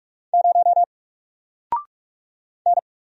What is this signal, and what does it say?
Morse code, heard as a keyed tone, about 44 wpm, 700 Hz. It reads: 0 N
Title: Single Letter-Number Alt 45wpm